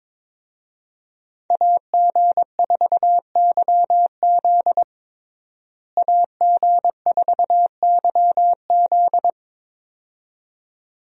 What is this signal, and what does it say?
Morse code, heard as a keyed tone, about 22 wpm, 700 Hz. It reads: AG4YZ AG4YZ
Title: USA Call Signs - 5 Characters - Repeated 2x - 22wpm